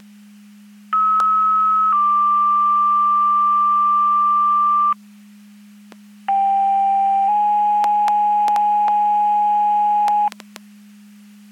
Two-tone page, heard as a one-second tone, then a three-second tone, about 1290 then 1180 Hz, about 790 then 830 Hz. It is heard from 0.9 to 10.3 s.
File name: Portage / 00082 10-03 Portage 154.385 Paging